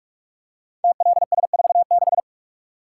Morse code, heard as a keyed tone, about 44 wpm, 700 Hz. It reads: TLS46
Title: Code Groups - 5 Letter:Numbers - 45wpm